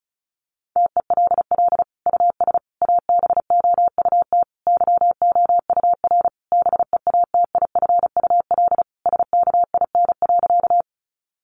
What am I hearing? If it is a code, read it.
Morse code, heard as a keyed tone, about 34 wpm, 700 Hz. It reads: TELL US ABOUT YOUR BEATIFUL SKIN.
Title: Sentences from Top 1300 Words 35wpm